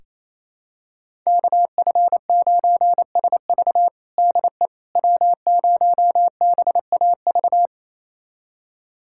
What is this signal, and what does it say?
Morse code, heard as a keyed tone, about 28 wpm, 700 Hz. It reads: KF9SV DE W0BAV
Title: USA Callsign DE Callsign - 5 Characters - 28wpm